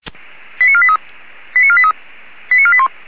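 Five-tone selective call (ZVEI), heard as quick runs of five tones, tones of about 70 ms. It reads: 89483 89483 89481
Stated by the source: ZVEI five-tone selective call 85860, received by SDR.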